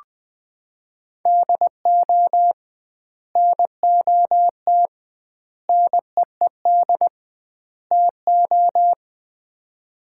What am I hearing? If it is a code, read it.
Morse code, heard as a keyed tone, about 20 wpm, 700 Hz. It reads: DO NOT NEED TO
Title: 2x Word-Spacing - Sets of 4 Words - 20wpm